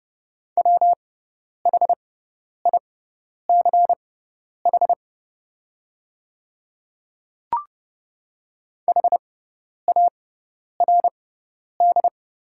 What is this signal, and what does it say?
Morse code, heard as a keyed tone, about 30 wpm, spaced as for 10 wpm, 700 Hz. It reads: WHICH HARD
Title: 6x Character Spacing - Top 300 Words - 30wpm